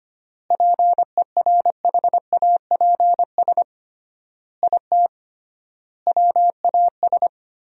Morse code, heard as a keyed tone, about 25 wpm, 700 Hz. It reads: PERHAPS IT WAS
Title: 3x Word-Spacing - Sentences from Top 500 Words - 25wpm